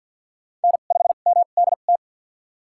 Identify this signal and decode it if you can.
Morse code, heard as a keyed tone, about 52 wpm, spaced as for 36 wpm, 700 Hz. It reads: N5MDT
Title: USA Call Signs - 5 Characters - 2x Character Spacing - 50wpm